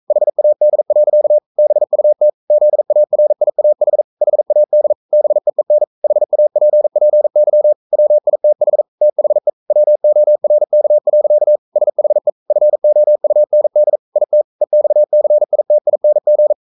Morse code, heard as a keyed tone, about 44 wpm, 600 Hz. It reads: HAD. BUT ZARIAH HAD BEEN HAPPY WITH THE WORK. SHE FOUND IT EXCITING
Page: Romanen Never af Ken Follet på engelsk.